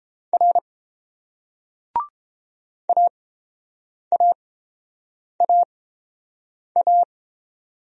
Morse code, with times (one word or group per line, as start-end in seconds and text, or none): 0.3-0.6 R
2.9-7.0 AAAA